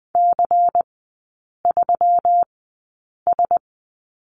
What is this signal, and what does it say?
Morse code, heard as a keyed tone, about 20 wpm, spaced as for 8 wpm, 700 Hz. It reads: C3S
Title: 121 Lesson 20e - Number 3 - Three Characters 20wpm